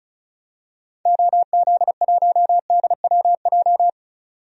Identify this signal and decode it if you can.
Morse code, heard as a keyed tone, about 36 wpm, 700 Hz. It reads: OZ1DWJ
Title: DXCC-100 Call Signs - 6 Characters - 35wpm